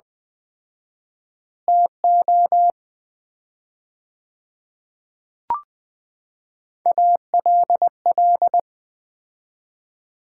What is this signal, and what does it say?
Morse code, heard as a keyed tone, about 21 wpm, 700 Hz. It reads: TO ALL
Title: 4x Word-Spacing - Sets of 2 Words - 20wpm